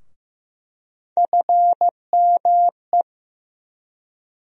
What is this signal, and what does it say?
Morse code, heard as a keyed tone, about 15 wpm, 700 Hz. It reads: FME
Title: USA Call Sign Postfixes - 3 Letters 15wpm